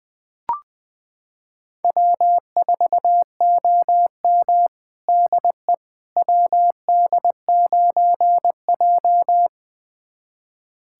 Morse code, heard as a keyed tone, about 21 wpm, 700 Hz. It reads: W4OM DE WD9J
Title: USA Callsign DE Callsign - 4 Characters - 20wpm